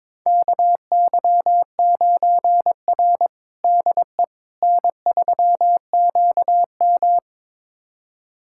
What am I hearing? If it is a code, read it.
Morse code, heard as a keyed tone, about 22 wpm, 700 Hz. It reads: KY9R DE N3QM
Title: USA Callsign DE Callsign - 4 Characters - 22wpm